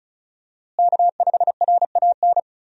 Morse code, heard as a keyed tone, about 35 wpm, 700 Hz. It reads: K5RAN
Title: USA Call Signs - 5 Characters - Repeated 3x - 35wpm